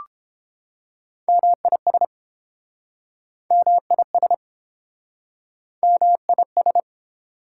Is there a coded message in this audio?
Yes: Morse code from 1.3 to 6.8 s, heard as a keyed tone, about 30 wpm, 700 Hz. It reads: MIS MIS MIS